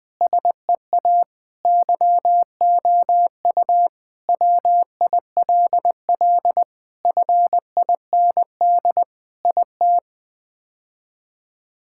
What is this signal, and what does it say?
Morse code, heard as a keyed tone, about 21 wpm, 700 Hz. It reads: SEA YOU WILL FIND IT
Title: Sentences with 3-Word Phrases - Spoken Beginning - 20wpm